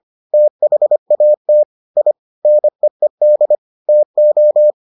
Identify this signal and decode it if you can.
Morse code, heard as a keyed tone, about 25 wpm, 600 Hz. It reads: THAT I NEED TO